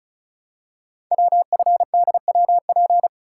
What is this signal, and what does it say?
Morse code, heard as a keyed tone, about 34 wpm, 700 Hz. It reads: WFDWP
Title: Code Groups - 5 Letters - 35wpm